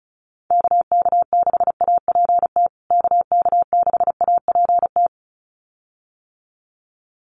Morse code, heard as a keyed tone, about 34 wpm, 700 Hz. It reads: KK6APT KK6APT
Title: USA Call Signs Repeated 2x 35wpm